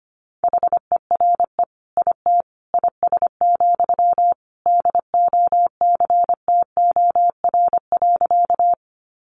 Morse code, heard as a keyed tone, about 25 wpm, 700 Hz. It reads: HERE IT IS, DOCTOR.
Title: Sentences from Top 1500 Words 25wpm